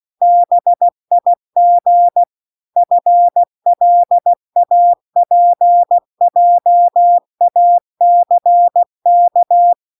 Morse code, heard as a keyed tone, about 16 wpm, 700 Hz. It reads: BIG FLAPJACK
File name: CW-The-Big-Twoheaded-River-2-16WPM